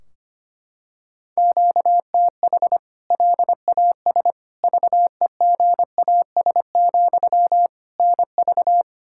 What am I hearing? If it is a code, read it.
Morse code, heard as a keyed tone, about 25 wpm, 700 Hz. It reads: QTH LAS VEGAS, NV
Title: K1USN SST QSO Practice 25wpm